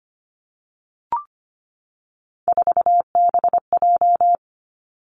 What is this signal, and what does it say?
Morse code, heard as a keyed tone, about 25 wpm, 700 Hz. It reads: 4BJ
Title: Three Challenging Letter-Number Combinations 25wpm